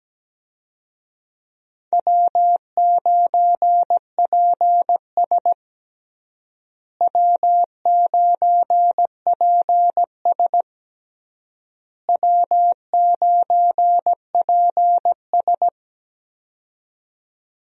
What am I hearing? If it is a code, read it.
Morse code, heard as a keyed tone, about 17 wpm, 700 Hz. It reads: W9PS W9PS W9PS
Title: USA Call Signs - 4 Characters - Repeated 3x - 17wpm